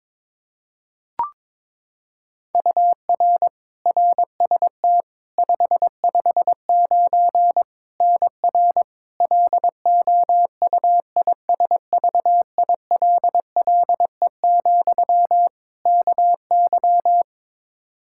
Morse code, heard as a keyed tone, about 22 wpm, 700 Hz. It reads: UR RST 559 NR LOUISVILLE, KY
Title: K1USN SST QSO Practice 22wpm